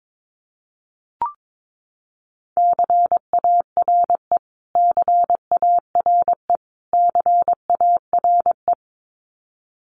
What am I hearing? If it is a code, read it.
Morse code, heard as a keyed tone, about 22 wpm, 700 Hz. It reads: CARE CARE CARE
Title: Top 300 Words Repeated 22wpm